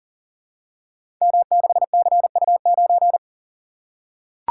Morse code, heard as a keyed tone, about 42 wpm, 700 Hz. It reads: M6CU9
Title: Code Groups - 5 Letter:Numbers - 40wpm